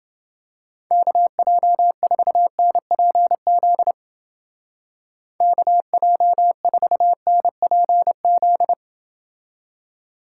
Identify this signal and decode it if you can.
Morse code, heard as a keyed tone, about 29 wpm, 700 Hz. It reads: KJ4NPZ KJ4NPZ